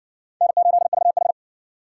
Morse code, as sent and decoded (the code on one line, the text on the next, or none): -. ---.. ...- ....
N8VH